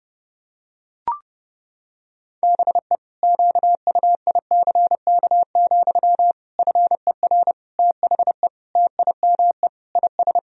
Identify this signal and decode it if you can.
Morse code, heard as a keyed tone, about 30 wpm, 700 Hz. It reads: BE QUICK, FER THE TIME IS
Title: Sentences from Top 500 Words and Abbreviations 30wpm